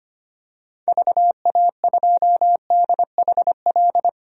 Morse code, heard as a keyed tone, about 25 wpm, 700 Hz. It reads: VA2DHL
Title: DXCC-100 Call Signs - 6 Characters - 25wpm